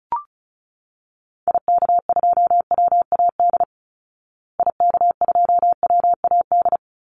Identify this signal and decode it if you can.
Morse code, heard as a keyed tone, about 34 wpm, 700 Hz. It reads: IK2WAD IK2WAD
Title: Call Signs Repeated 3x - 35wpm